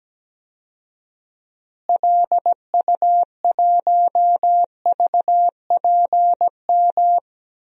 Morse code, heard as a keyed tone, about 17 wpm, 700 Hz. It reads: LU1VPM
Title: DXCC-100 Call Signs - 6 Characters - Repeated 2x - 17wpm